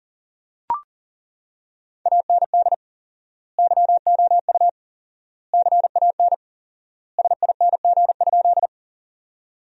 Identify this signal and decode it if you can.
Morse code, heard as a keyed tone, about 40 wpm, 700 Hz. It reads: AND YOU CAN SING?
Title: 4x Word-Spacing - Sentences from Top 400 Words - 40wpm